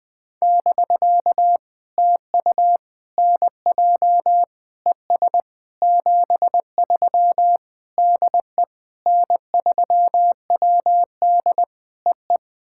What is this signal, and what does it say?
Morse code, heard as a keyed tone, about 20 wpm, 700 Hz. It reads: <BK> TU NJ ES 73 DE N3WD EE
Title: POTA QSO Practice - 20wpm